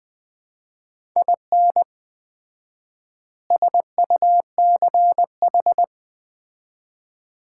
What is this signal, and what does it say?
Morse code, heard as a keyed tone, about 20 wpm, 700 Hz. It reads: IN SUCH
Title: 4x Word-Spacing - Sets of 5 Words - 20wpm